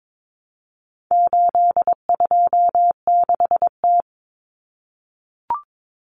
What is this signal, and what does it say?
Morse code, heard as a keyed tone, about 22 wpm, 700 Hz. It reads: 826T
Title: Common Cut Numbers - 4-Digit 22wpm